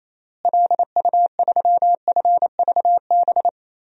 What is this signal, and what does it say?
Morse code, heard as a keyed tone, about 28 wpm, 700 Hz. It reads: LU3FVB
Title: DXCC-100 Call Signs - 6 Characters - Repeated 3x - 28wpm